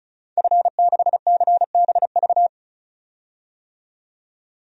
Morse code, heard as a keyed tone, about 35 wpm, 700 Hz. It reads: F6CBV
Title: DXCC-100 Call Signs - 5 Characters - 35wpm